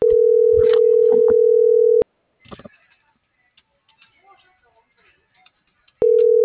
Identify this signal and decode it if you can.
Ringback tone (North America), 440 + 480 Hz.
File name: CENTRAL BLN / out marco.llanos